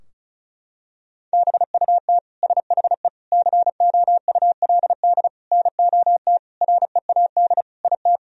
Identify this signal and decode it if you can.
Morse code, heard as a keyed tone, about 34 wpm, 700 Hz. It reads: BUT SHE COULD NOT READ IT
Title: Sentences from Top 1000 Words - Review Part 4 - 35wpm